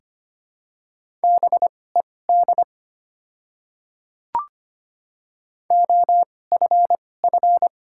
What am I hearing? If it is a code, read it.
Morse code, heard as a keyed tone, about 25 wpm, spaced as for 18 wpm, 700 Hz. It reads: BED OFF
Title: 2x Character Spacing - Top 500 Words - 25wpm